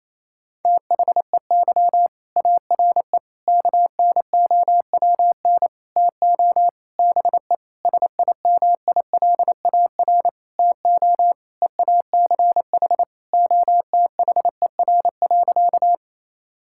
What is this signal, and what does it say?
Morse code, heard as a keyed tone, about 28 wpm, 700 Hz. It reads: THEY ARE KNOWN TO BE SIMILAR TO EACH OTHER.
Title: Sentences from Top 1000 Words - Review Part 5 - 28wpm